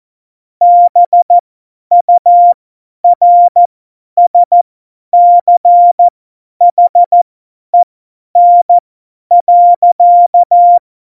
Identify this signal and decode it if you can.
Morse code, heard as a keyed tone, about 14 wpm, spaced as for 10 wpm, 700 Hz. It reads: BURSCHEN.
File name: Morsekurs / Witze / Witz 01. Dezember 2020, 50 F70